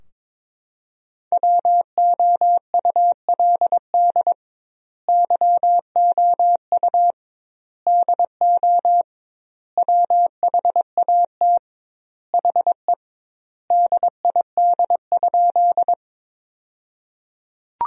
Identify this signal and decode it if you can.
Morse code, heard as a keyed tone, about 22 wpm, 700 Hz. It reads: WOULD YOU DO WHAT HE DID?
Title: 2x Word-Spacing - Sentences from Top 100 Words - 22wpm